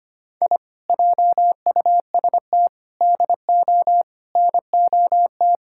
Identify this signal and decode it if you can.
Morse code, heard as a keyed tone, about 25 wpm, 700 Hz. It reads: I JUST DO NOT
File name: practice / 25wpm / Sentences with 4-Word Phrases - 25wpm